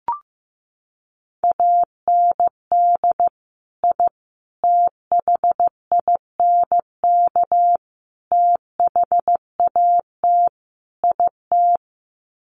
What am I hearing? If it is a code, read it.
Morse code, heard as a keyed tone, about 15 wpm, 700 Hz. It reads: AND I THINK THAT IT
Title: Sets of 5 Words Encore2 15wpm